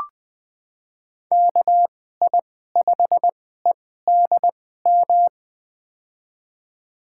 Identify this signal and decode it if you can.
Morse code, heard as a keyed tone, about 20 wpm, spaced as for 14 wpm, 700 Hz. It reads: KI5EDM
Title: USA Call Signs - 2x Character Spacing - 20wpm